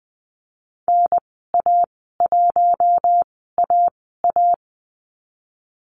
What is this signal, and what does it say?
Morse code, heard as a keyed tone, about 19 wpm, spaced as for 14 wpm, 700 Hz. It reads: NA1AA
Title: 055 Lesson 10g - Letter r - Call Signs 20wpm